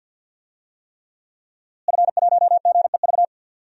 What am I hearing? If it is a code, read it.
Morse code, heard as a keyed tone, about 49 wpm, 700 Hz. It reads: F1GEV
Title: DXCC-100 Call Signs - 5 Characters - Repeated 3x - 50wpm